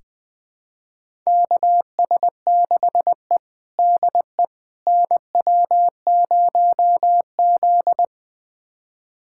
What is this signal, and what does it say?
Morse code, heard as a keyed tone, about 20 wpm, 700 Hz. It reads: KS6E DE NW0Z
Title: USA Callsign DE Callsign - 4 Characters - 20wpm